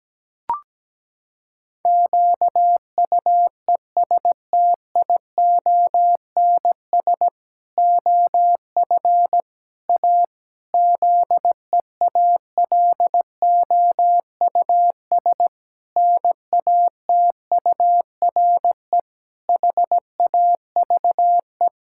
Morse code, heard as a keyed tone, about 17 wpm, 700 Hz. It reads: QUESTIONS OF A ZEALOUS NATURE HAVE
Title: Pangram Challenge v2 - 17wpm